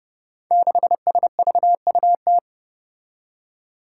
Morse code, heard as a keyed tone, about 28 wpm, 700 Hz. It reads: 6SVUT
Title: Five Challenging Letter-Number Combinations 30wpm